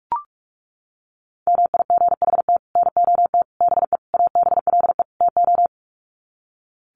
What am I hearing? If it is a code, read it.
Morse code, heard as a keyed tone, about 44 wpm, 700 Hz. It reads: MIGHT NOT BE ABLE TO
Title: Sets of 5 Words Encore2 45wpm